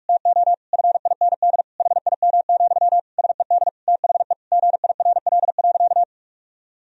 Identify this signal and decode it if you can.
Morse code, heard as a keyed tone, about 46 wpm, 700 Hz. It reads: TO FIND HIM, SED THE GIRL.
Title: Sentences from Top 400 Words and Abbreviations 45wpm